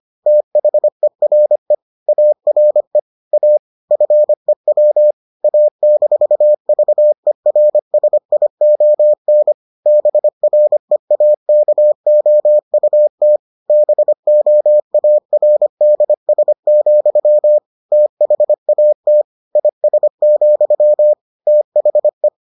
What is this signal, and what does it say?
Morse code, heard as a keyed tone, about 25 wpm, 600 Hz. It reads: THERE ARE A FEW A-VERSION BREAKOUT BOARDS, THAT IS, THE